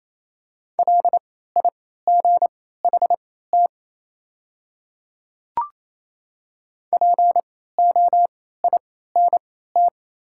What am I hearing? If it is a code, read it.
Morse code, heard as a keyed tone, about 28 wpm, spaced as for 16 wpm, 700 Hz. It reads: LIGHT POINT
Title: 3x Character Spacing - Top 300 Words - 28wpm